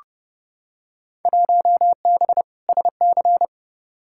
Morse code, heard as a keyed tone, about 31 wpm, 700 Hz. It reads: 1B SC
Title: Field Day Exchanges Encore - 30wpm